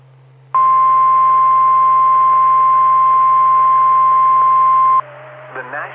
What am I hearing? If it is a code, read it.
Emergency Alert System (SAME) tones (the 1050 Hz attention tone), with speech. It is heard from 0.5 to 5.1 s.